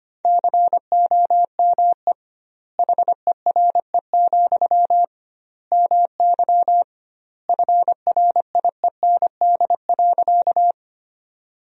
Morse code, heard as a keyed tone, about 25 wpm, 700 Hz. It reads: COME HERE, MY FRIEND.
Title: 2x Word-Spacing - Sentences from Top 300 Words - 25wpm